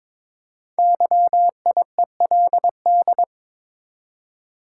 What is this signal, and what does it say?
Morse code, heard as a keyed tone, about 22 wpm, 700 Hz. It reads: YIELD
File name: 5 Letter Words - First 3 Letters Given - 22wpm